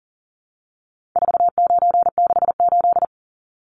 Morse code, heard as a keyed tone, about 40 wpm, 700 Hz. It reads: 4968
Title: Numbers - 4-Digit 40wpm